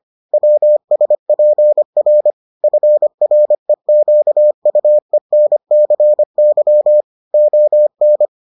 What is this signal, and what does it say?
Morse code, heard as a keyed tone, about 25 wpm, 600 Hz. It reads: WSPR FREQUENCY ON